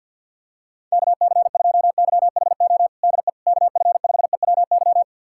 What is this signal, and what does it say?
Morse code, heard as a keyed tone, about 52 wpm, 700 Hz. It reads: KC2YHO DE KF5EWY
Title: USA Callsign DE Callsign - 6 Characters - 50wpm